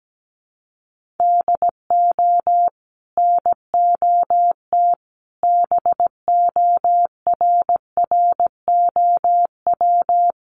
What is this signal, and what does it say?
Morse code, heard as a keyed tone, about 17 wpm, 700 Hz. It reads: DO NOT BORROW
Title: Sentences from Top 1500 Words 17wpm